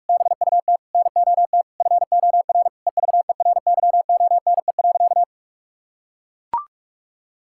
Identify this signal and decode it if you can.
Morse code, heard as a keyed tone, about 42 wpm, 700 Hz. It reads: BUT NOT FOR EVERYONE.
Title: Sentences from Top 1000 Words - Review Part 5 - 45wpm